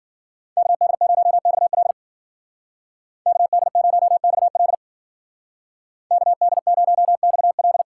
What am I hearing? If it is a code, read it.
Morse code, heard as a keyed tone, about 50 wpm, 700 Hz. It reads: KD0XL KD0XL KD0XL
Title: Speed Racing - USA 2x2 Call Signs 40wpm